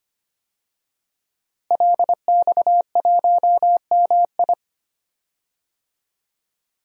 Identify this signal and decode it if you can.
Morse code, heard as a keyed tone, about 25 wpm, 700 Hz. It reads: LX1MI
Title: DXCC-100 Call Signs - 5 Characters - Repeated 2x - 25wpm